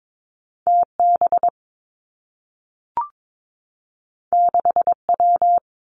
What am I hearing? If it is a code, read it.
Morse code, heard as a keyed tone, about 22 wpm, 700 Hz. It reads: T B 6 W
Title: Two Challenging Letter-Number Combinations 22wpm